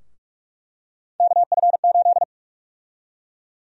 Morse code, heard as a keyed tone, about 42 wpm, 700 Hz. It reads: KF8
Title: USA Call Sign Prefixes - Repeated 2x - 45wpm